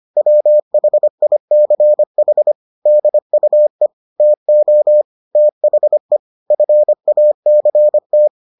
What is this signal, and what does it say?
Morse code, heard as a keyed tone, about 25 wpm, 600 Hz. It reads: WHICH DUE TO THE FACT